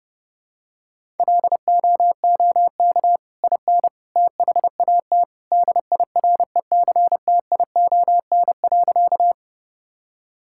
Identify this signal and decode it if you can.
Morse code, heard as a keyed tone, about 28 wpm, 700 Hz. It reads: LOOK IN THAT DIRECTION.
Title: Sentences from Top 1000 Words - Review Part 5 - 30wpm